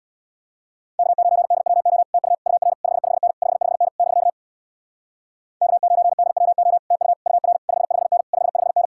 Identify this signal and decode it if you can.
Morse code, heard as a keyed tone, about 65 wpm, 700 Hz. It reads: K9DWO TU UR 55N 55N <BK> K9DWO TU UR 55N 55N